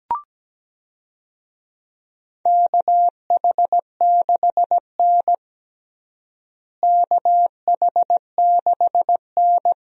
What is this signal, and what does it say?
Morse code, heard as a keyed tone, about 17 wpm, 700 Hz. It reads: KH6N KH6N
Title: DXCC-100 Call Signs - 4 Characters - Repeated 2x - 17wpm